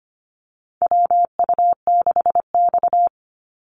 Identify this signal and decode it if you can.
Morse code, heard as a keyed tone, about 25 wpm, 700 Hz. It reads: WU6X
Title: Call Signs Repeated 2x - 25wpm